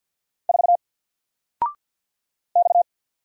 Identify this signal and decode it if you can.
Morse code, heard as a keyed tone, about 48 wpm, 700 Hz. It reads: 4 X
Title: Single Letter-Number Alt 50wpm